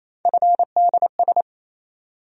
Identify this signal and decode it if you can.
Morse code, heard as a keyed tone, about 27 wpm, 700 Hz. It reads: FDS
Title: USA Call Sign Postfixes - 3 Letters 28wpm